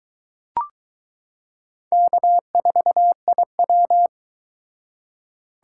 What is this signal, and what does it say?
Morse code, heard as a keyed tone, about 21 wpm, 700 Hz. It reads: K4IW